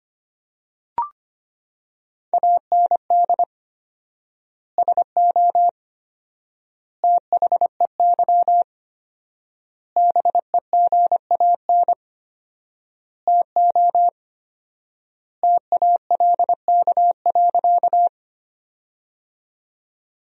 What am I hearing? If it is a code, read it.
Morse code, heard as a keyed tone, about 25 wpm, 700 Hz. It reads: AND SO THEY BEGAN TO TALK.
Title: 4x Word-Spacing - Sentences from Top 400 Words - 25wpm